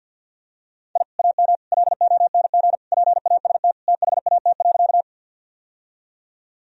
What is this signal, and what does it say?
Morse code, heard as a keyed tone, about 48 wpm, 700 Hz. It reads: I AM LONG PAST THAT.
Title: Sentences from Top 1000 Words - Review Part 3 - 50wpm